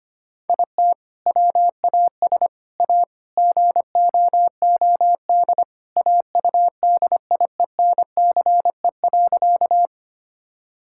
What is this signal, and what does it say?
Morse code, heard as a keyed tone, about 25 wpm, 700 Hz. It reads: IT WAS A GOOD AUDIENCE.
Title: Sentences from Top 1000 Words - Review Part 5 - 25wpm